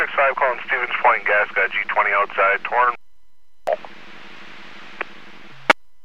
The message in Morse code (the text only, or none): none